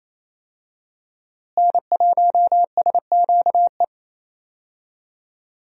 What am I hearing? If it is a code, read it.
Morse code, heard as a keyed tone, about 28 wpm, 700 Hz. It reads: N1SQE
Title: USA Call Signs - 5 Characters - Repeated 2x - 28wpm